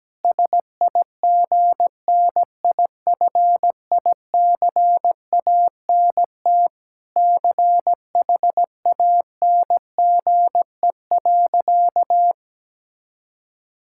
Morse code, heard as a keyed tone, about 17 wpm, 700 Hz. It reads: SIGNIFICANT CHANGE.
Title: Sentences from Top 1000 Words - Review Part 6 - 17wpm